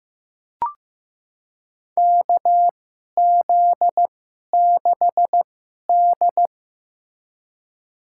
Morse code, heard as a keyed tone, about 15 wpm, spaced as for 11 wpm, 700 Hz. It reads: KZ6D
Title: USA Call Signs - 4 Characters - 2x Character Spacing - 15wpm